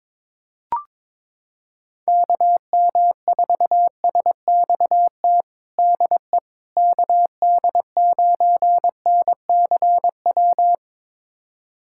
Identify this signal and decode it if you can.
Morse code, heard as a keyed tone, about 22 wpm, 700 Hz. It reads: KM4SXT DE KD9NCW
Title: USA Callsign DE Callsign - 6 Characters - 22wpm